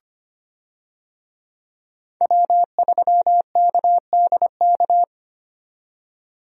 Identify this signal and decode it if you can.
Morse code, heard as a keyed tone, about 25 wpm, 700 Hz. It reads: W3KDK
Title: USA Call Signs - 5 Characters - 25wpm